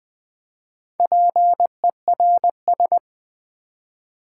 Morse code, heard as a keyed tone, about 20 wpm, 700 Hz. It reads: PERS
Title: Build A Word - 4 Letters At A Time - 11-Letter Words - 20wpm